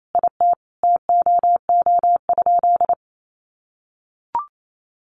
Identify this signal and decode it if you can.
Morse code, heard as a keyed tone, about 28 wpm, 700 Hz. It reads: IT TOO?
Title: Sentences from Top 1100 Words 28wpm